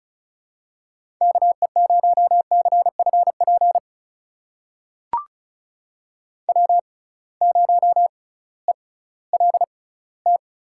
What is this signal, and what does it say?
Morse code, heard as a keyed tone, about 36 wpm, spaced as for 19 wpm, 700 Hz. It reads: KE0CFP W0ELT
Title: USA Call Signs - 6x Character Spacing - 35wpm